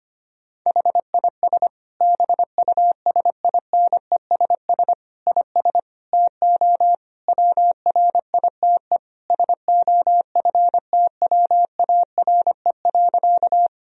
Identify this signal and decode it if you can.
Morse code, heard as a keyed tone, about 25 wpm, 700 Hz. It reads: HIS BUSINESS IS TO WRITE SOFTWARE.
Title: Sentences from Top 1000 Words - Review Part 3 - 25wpm